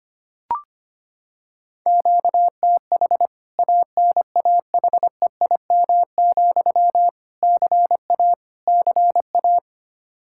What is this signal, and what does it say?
Morse code, heard as a keyed tone, about 25 wpm, 700 Hz. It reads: QTH ANAHEIM, CA CA
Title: K1USN SST QSO Practice 25wpm